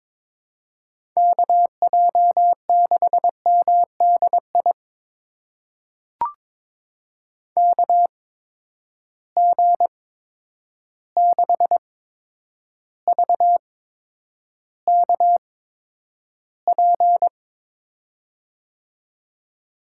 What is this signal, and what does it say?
Morse code, heard as a keyed tone, about 23 wpm, spaced as for 10 wpm, 700 Hz. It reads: KJ6MDI KG6VKP
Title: USA Call Signs - 6 Characters - 8x Character Spacing - 22wpm